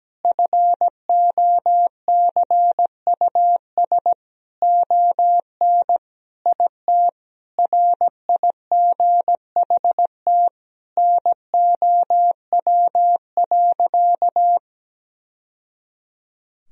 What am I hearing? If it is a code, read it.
Morse code, heard as a keyed tone, about 17 wpm, 700 Hz. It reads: FOCUS ON IT RIGHT NOW.
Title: Sentences from Top 1000 Words - Review Part 4 - 17wpm